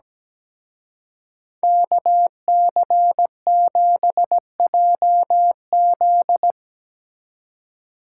Morse code, heard as a keyed tone, about 19 wpm, 700 Hz. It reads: KC7JZ
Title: Speed Racing - USA 2x2 Call Signs 17wpm